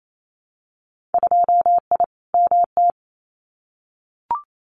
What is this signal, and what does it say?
Morse code, heard as a keyed tone, about 28 wpm, 700 Hz. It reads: 2I MT
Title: Winter Field Day Exchanges Encore 28wpm